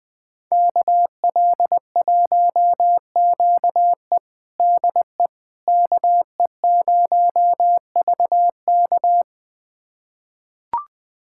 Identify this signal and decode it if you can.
Morse code, heard as a keyed tone, about 20 wpm, 700 Hz. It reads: KL1QE DE KE0VK
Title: USA Callsign DE Callsign - 5 Characters - 20wpm